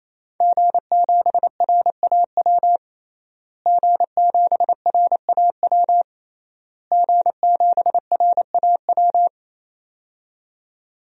Morse code, heard as a keyed tone, about 27 wpm, 700 Hz. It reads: G7RAW G7RAW G7RAW
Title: DXCC-100 Call Signs - 5 Characters - Repeated 3x - 28wpm